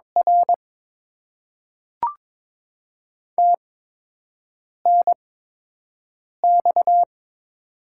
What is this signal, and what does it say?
Morse code, heard as a keyed tone, about 22 wpm, 700 Hz. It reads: R T N X
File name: Top 90 QSO Elements - 8x Character Spacing - 22wpm